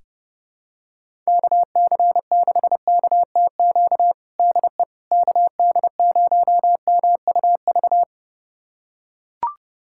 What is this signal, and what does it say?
Morse code, heard as a keyed tone, about 32 wpm, 700 Hz. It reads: KC6KTQ DE KD0MUV
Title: USA Callsign DE Callsign 30wpm